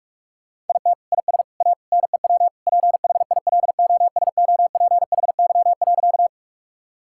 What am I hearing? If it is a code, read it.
Morse code, heard as a keyed tone, about 41 wpm, 700 Hz. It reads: IT IS A NEW PHILOSOPHY.
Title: Sentences from Top 1000 Words - Review Part 6 - 45wpm